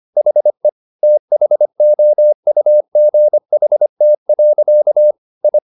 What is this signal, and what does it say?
Morse code, heard as a keyed tone, about 25 wpm, 600 Hz. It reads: HE THOUGHT. I